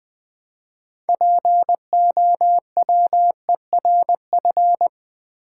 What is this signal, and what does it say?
Morse code, heard as a keyed tone, about 20 wpm, 700 Hz. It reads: POWERF
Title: Build A Word - 3 Letters At A Time - 8-Letter Words - 20wpm